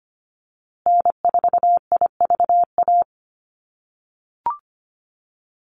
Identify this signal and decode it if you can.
Morse code, heard as a keyed tone, about 25 wpm, 700 Hz. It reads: N4IVA
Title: USA Call Signs Repeated 2x 25wpm